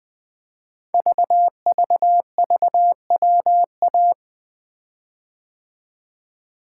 Morse code, heard as a keyed tone, about 20 wpm, 700 Hz. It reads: VVVWA
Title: Code Groups - 5 Letters - 20wpm